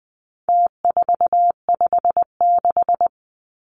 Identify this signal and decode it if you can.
Morse code, heard as a keyed tone, about 20 wpm, 700 Hz. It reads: T456
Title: Numbers - Contest Style - 4-Digit 20wpm